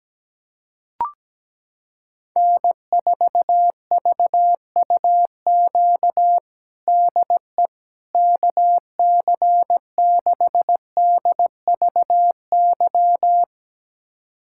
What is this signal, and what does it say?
Morse code, heard as a keyed tone, about 17 wpm, 700 Hz. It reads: N4VUQ DE KC6DVY
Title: USA Callsign DE Callsign 17wpm